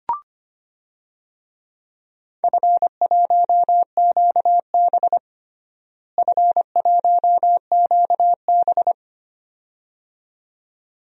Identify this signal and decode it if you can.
Morse code, heard as a keyed tone, about 25 wpm, 700 Hz. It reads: F1QB F1QB
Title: DXCC-100 Call Signs - 4 Characters - Repeated 2x - 25wpm